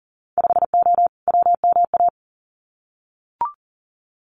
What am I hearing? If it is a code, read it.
Morse code, heard as a keyed tone, about 40 wpm, 700 Hz. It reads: 5O WMA
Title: Winter Field Day Exchanges 40wpm